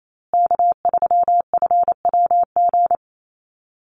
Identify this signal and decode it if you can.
Morse code, heard as a keyed tone, about 28 wpm, 700 Hz. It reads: K3FWG
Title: USA Call Signs Repeated 3x 28wpm